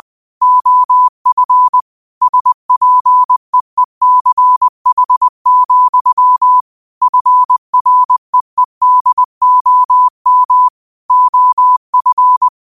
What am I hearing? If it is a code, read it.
Morse code, heard as a keyed tone, about 20 wpm, 1000 Hz. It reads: OF SPEECH, FREEDOM OF